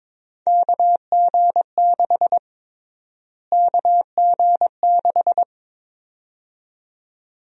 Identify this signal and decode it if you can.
Morse code, heard as a keyed tone, about 22 wpm, 700 Hz. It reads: KG6 KG6
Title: USA Call Sign Prefixes - Repeated 2x - 22wpm